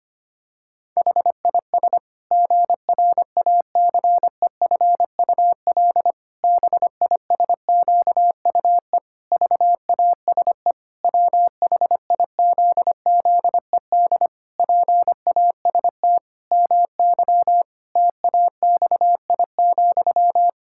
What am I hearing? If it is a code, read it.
Morse code, heard as a keyed tone, about 25 wpm, 700 Hz. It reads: HIS GRACEFUL BISQUE VASE WHIZZED PAST MY TAXI,
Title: Pangram Challenge v2 - 25wpm